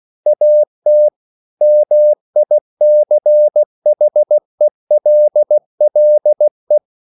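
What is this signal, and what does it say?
Morse code, heard as a keyed tone, about 16 wpm, 600 Hz. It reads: AT MICHELLE
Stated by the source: Slow (QRS) Morse code at 16 wpm, from CW med Gnister.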